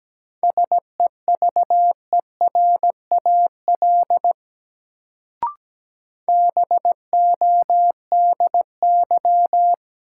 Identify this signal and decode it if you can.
Morse code, heard as a keyed tone, about 17 wpm, 700 Hz. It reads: SEVERAL BODY
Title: Sentences from Top 400 Words - New Word Drill - 17wpm